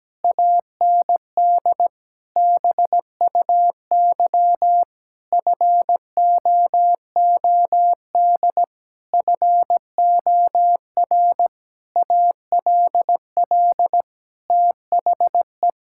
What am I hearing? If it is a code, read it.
Morse code, heard as a keyed tone, about 17 wpm, 700 Hz. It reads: AND BUY FOOD FOR ALL THE
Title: Sentences from Top 1600 Words 17wpm